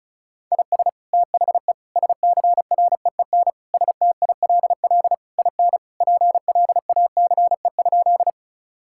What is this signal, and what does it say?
Morse code, heard as a keyed tone, about 34 wpm, 700 Hz. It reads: IS THE SCREEN STILL IN PLACE?
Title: Sentences from Top 1000 Words - Review Part 6 - 35wpm